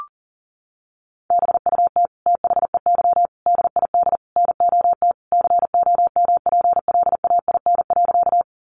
Morse code, heard as a keyed tone, about 40 wpm, 700 Hz. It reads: BUT THEY DID NOT COMPLAIN.
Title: Sentences from Top 1300 Words 40wpm